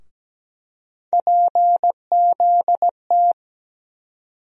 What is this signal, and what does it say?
Morse code, heard as a keyed tone, about 17 wpm, 700 Hz. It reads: PZT
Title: USA Call Sign Postfixes - 3 Letters 17wpm